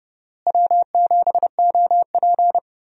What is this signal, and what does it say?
Morse code, heard as a keyed tone, about 32 wpm, 700 Hz. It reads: W7OP
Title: USA Call Signs - 4 Characters - Repeated 3x - 30wpm